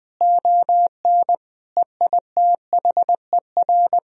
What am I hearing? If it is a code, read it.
Morse code, heard as a keyed tone, about 20 wpm, 700 Hz. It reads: ON EITHER
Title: Sentences from Top 1000 Words - Review Part 4 - 20wpm